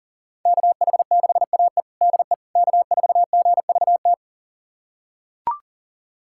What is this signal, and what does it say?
Morse code, heard as a keyed tone, about 40 wpm, 700 Hz. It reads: KH6AE DE K4GVT
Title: USA Callsign DE Callsign - 5 Characters - 40wpm